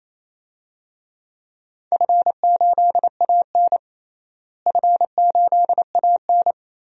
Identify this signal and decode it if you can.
Morse code, heard as a keyed tone, about 28 wpm, 700 Hz. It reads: F8AN F8AN
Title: DXCC-100 Call Signs - 4 Characters - Repeated 2x - 28wpm